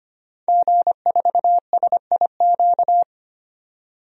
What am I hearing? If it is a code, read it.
Morse code, heard as a keyed tone, about 25 wpm, 700 Hz. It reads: G4SIQ
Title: DXCC-100 Call Signs - 5 Characters - 25wpm